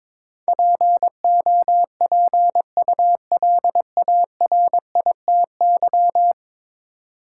Morse code, heard as a keyed tone, about 22 wpm, 700 Hz. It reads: POPULARITY
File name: practice / 22wpm / 10 Letter Words - First 3 Letters Given - 22wpm